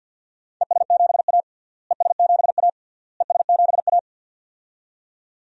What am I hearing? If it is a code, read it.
Morse code, heard as a keyed tone, about 50 wpm, 700 Hz. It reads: ES7A ES7A ES7A
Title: DXCC-100 Call Signs - 4 Characters - Repeated 3x - 50wpm